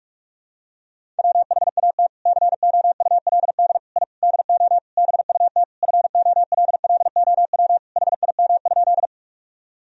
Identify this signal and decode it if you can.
Morse code, heard as a keyed tone, about 46 wpm, 700 Hz. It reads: WHAT COULD I DO BUT FOLLOW HIM?
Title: Sentences from Top 1000 Words - Review Part 1 - 45wpm